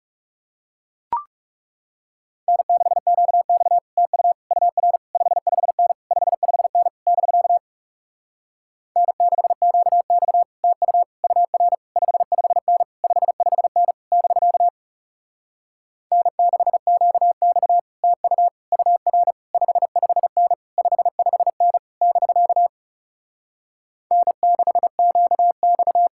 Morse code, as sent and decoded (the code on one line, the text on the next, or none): -. -.... --.- -..- / - ..- / ..- .-. / ..... ..... -. / ..... ..... -. / -...-.- / -. -.... --.- -..- / - ..- / ..- .-. / ..... ..... -. / ..... ..... -. / -...-.- / -. -.... --.- -..- / - ..- / ..- .-. / ..... ..... -. / ..... ..... -. / -...-.- / -. -.... --.- -..-
N6QX TU UR 55N 55N <BK> N6QX TU UR 55N 55N <BK> N6QX TU UR 55N 55N <BK> N6QX